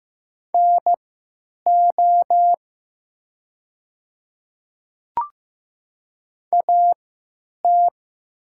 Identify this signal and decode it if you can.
Morse code, heard as a keyed tone, about 15 wpm, spaced as for 8 wpm, 700 Hz. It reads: NO AT
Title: 3x Character Spacing - Top 100 Words - 15wpm